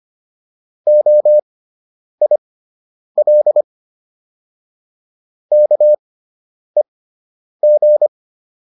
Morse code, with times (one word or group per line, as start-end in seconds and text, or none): 0.9-3.6 OIL
5.5-8.1 KEG